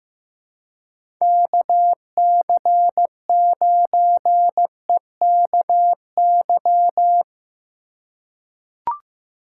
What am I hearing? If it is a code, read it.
Morse code, heard as a keyed tone, about 15 wpm, 700 Hz. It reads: KC9EKY
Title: USA Call Signs - 6 Characters - 3x Character Spacing - 15wpm